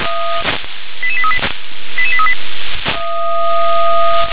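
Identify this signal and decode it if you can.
Five-tone selective call (ZVEI), heard as quick runs of five tones, tones of about 70 ms. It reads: 88838 88838
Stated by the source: ZVEI five-tone call received by SDR.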